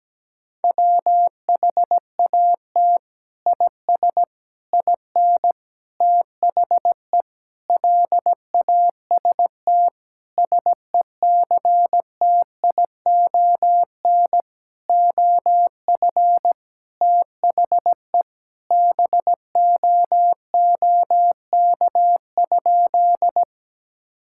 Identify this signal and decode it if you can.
Morse code, heard as a keyed tone, about 17 wpm, 700 Hz. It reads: WHAT IS IN THE LAST SECTION OF THE BOOK?
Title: Sentences from Top 1000 Words - Review Part 5 - 17wpm